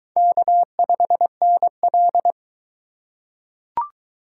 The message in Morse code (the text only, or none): K5NL